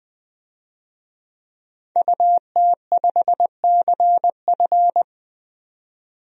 Morse code, heard as a keyed tone, about 20 wpm, 700 Hz. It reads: UT5CF
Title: DXCC-100 Call Signs - 5 Characters - 20wpm